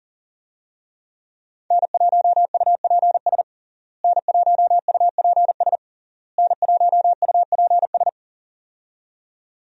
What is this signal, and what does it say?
Morse code, heard as a keyed tone, about 37 wpm, 700 Hz. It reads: N1UPS N1UPS N1UPS
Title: USA Call Signs - 5 Characters - Repeated 3x - 40wpm